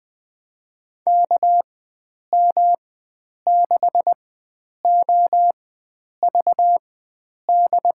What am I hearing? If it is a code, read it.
Morse code, heard as a keyed tone, about 21 wpm, spaced as for 9 wpm, 700 Hz. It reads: KM6OVD
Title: USA Call Signs - 6 Characters - 4x Character Spacing - 20wpm